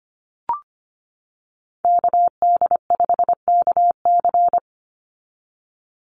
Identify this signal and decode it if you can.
Morse code, heard as a keyed tone, about 25 wpm, 700 Hz. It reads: KD5KC
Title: NA SOTA Activator Callsigns 25wpm